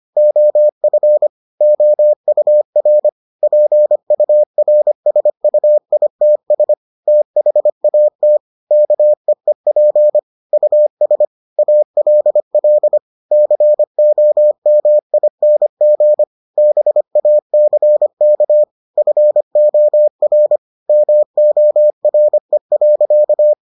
Morse code, heard as a keyed tone, about 25 wpm, 600 Hz. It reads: OF OUR PURSUITS THAT KEEP US ALL COMING BACK FOR MORE.